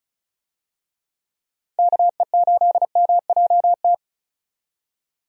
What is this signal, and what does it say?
Morse code, heard as a keyed tone, about 34 wpm, 700 Hz. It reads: KE8MJT
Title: USA Call Signs - 6 Characters - 35wpm